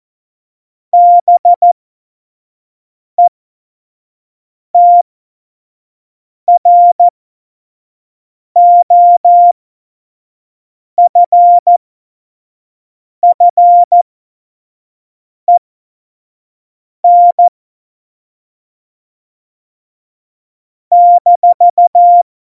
Morse code, heard as a keyed tone, about 14 wpm, spaced as for 5 wpm, 700 Hz. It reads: BETROFFEN -
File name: Morsekurs / Witze / Witz 29. Dezember 2020, 25 F70